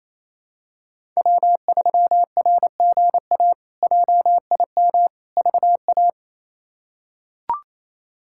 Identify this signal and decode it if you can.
Morse code, heard as a keyed tone, about 28 wpm, 700 Hz. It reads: W3RGA JIM VA
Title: K1USN SST QSO Practice - 28wpm